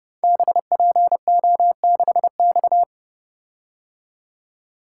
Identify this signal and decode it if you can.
Morse code, heard as a keyed tone, about 30 wpm, 700 Hz. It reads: BPO6X
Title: Code Groups - 5 Letter:Numbers - 30wpm